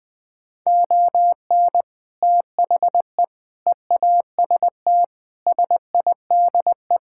Morse code, heard as a keyed tone, about 20 wpm, 700 Hz. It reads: ON THE EAST SIDE
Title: Sets of 6 Words - 20wpm